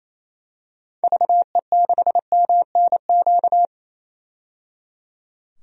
Morse code, heard as a keyed tone, about 28 wpm, 700 Hz. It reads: VE6MNQ
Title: DXCC-100 Call Signs - 6 Characters - 28wpm